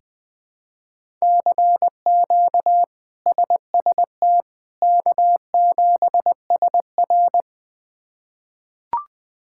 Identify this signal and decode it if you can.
Morse code, heard as a keyed tone, about 20 wpm, 700 Hz. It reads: CQ SST K7SR
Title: K1USN SST QSO Practice - 20wpm